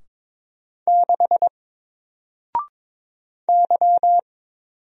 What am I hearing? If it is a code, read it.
Morse code, heard as a keyed tone, about 22 wpm, 700 Hz. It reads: 6 Y
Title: Single Letter-Number 22wpm